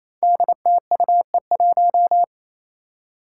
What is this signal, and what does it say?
Morse code, heard as a keyed tone, about 28 wpm, 700 Hz. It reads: DTUE1
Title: Five Challenging Letter-Number Combinations 28wpm